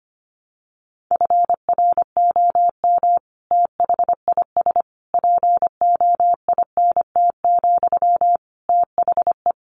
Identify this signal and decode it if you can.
Morse code, heard as a keyed tone, about 25 wpm, 700 Hz. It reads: FROM THIS POINT, THE
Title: Sentences from Top 1200 Words 25wpm